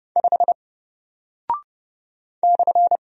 Morse code, heard as a keyed tone, about 30 wpm, 700 Hz. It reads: 5 /
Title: Single Character (Letter:Number:Punctuation:Prosign) Alt 30wpm